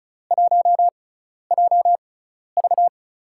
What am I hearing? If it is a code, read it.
Morse code, heard as a keyed tone, about 36 wpm, spaced as for 12 wpm, 700 Hz. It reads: 1JV
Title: USA Call Signs - 4 Characters - 6x Character Spacing - 35wpm